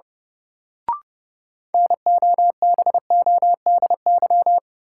Morse code, heard as a keyed tone, about 30 wpm, 700 Hz. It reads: NOBODY
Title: Sentences from Top 1600 Words - New Word Drill - 30wpm